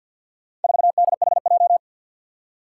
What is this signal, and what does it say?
Morse code, heard as a keyed tone, about 52 wpm, 700 Hz. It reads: 4DHJ
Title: Four Challenging Letter-Number Combinations 50wpm